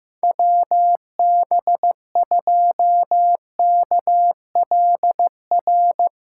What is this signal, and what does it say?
Morse code, heard as a keyed tone, about 15 wpm, 700 Hz. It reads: WB2KLR
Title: USA Callsign DE Callsign 15wpm